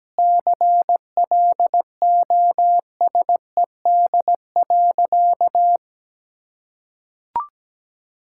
Morse code, heard as a keyed tone, about 17 wpm, 700 Hz. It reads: CLOSED.
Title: Sentences from Top 1600 Words 17wpm